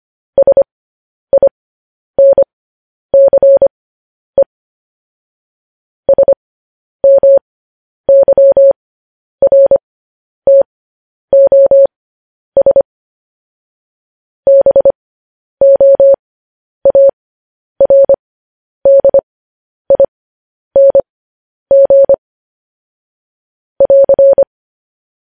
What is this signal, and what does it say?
Morse code, heard as a keyed tone, about 25 wpm, spaced as for 10 wpm, 550 Hz. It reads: SINCE SMYRTOS BOARDING <AR>
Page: news-headlines-in-morse-code-at-10-wpm